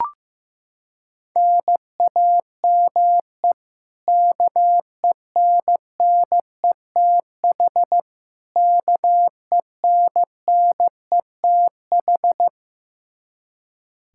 Morse code, heard as a keyed tone, about 15 wpm, 700 Hz. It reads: NAME KENNETH KENNETH
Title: Practice QSO - Exchange 1 - Beginning - 15wpm